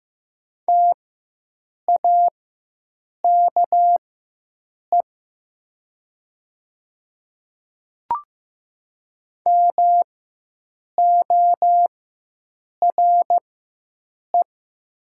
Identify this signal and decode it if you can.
Morse code, heard as a keyed tone, about 15 wpm, spaced as for 7 wpm, 700 Hz. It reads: TAKE MORE
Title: 4x Character Spacing - Top 200 Words - 15wpm